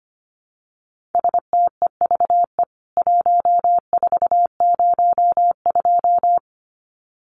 Morse code, heard as a keyed tone, about 25 wpm, 700 Hz. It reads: STEVE 1402
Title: CWT Exchange 25wpm